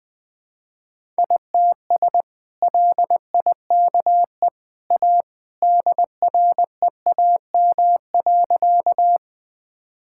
Morse code, heard as a keyed tone, about 19 wpm, 700 Hz. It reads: ITS LIKE A DREAM.
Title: Sentences with 2-Word Phrases - 20wpm